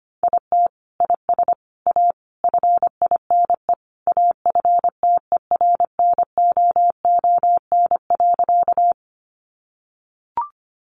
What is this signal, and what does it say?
Morse code, heard as a keyed tone, about 25 wpm, 700 Hz. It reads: IT IS A FINE AFTERNOON.
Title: Sentences from Top 1500 Words 25wpm